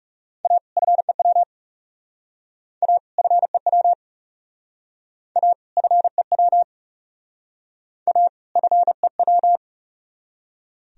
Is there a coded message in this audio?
Yes: Morse code from 0.5 to 9.6 s, heard as a keyed tone, about 39 wpm, 700 Hz. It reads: A FEW A FEW A FEW A FEW